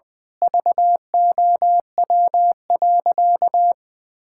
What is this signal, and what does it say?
Morse code, heard as a keyed tone, about 20 wpm, 700 Hz. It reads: VOW.
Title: Pangram Challenge v2 - 20wpm